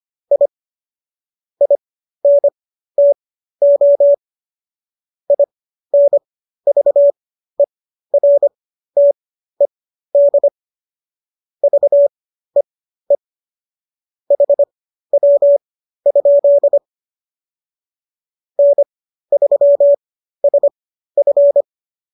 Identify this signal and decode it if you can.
Morse code, heard as a keyed tone, about 25 wpm, spaced as for 13 wpm, 570 Hz. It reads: I INTO INVERTED VEE HW? N3SF